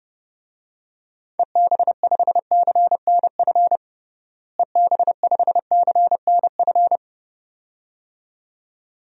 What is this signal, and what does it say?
Morse code, heard as a keyed tone, about 30 wpm, 700 Hz. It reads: EB5CNF EB5CNF
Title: DXCC-100 Call Signs - 6 Characters - Repeated 2x - 30wpm